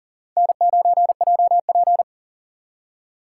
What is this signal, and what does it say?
Morse code, heard as a keyed tone, about 52 wpm, 700 Hz. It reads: N9JP